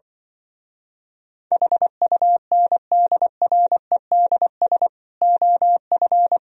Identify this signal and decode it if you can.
Morse code, heard as a keyed tone, about 24 wpm, 700 Hz. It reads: HUNDREDS OF
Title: Speed-Racing - Sets of 5 Words 20wpm